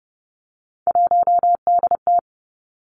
Morse code, heard as a keyed tone, about 30 wpm, 700 Hz. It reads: 1DT
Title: Three Challenging Letter-Number Combinations 30wpm